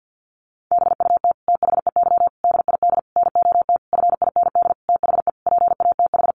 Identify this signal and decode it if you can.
Morse code, heard as a keyed tone, about 48 wpm, 700 Hz. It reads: BUT THEY DID NOT FIND THE PATH
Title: Sentences from Top 1400 Words 50wpm